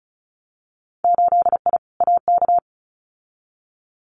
Morse code, heard as a keyed tone, about 36 wpm, 700 Hz. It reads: 8I AK
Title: Winter Field Day Exchanges Encore 35wpm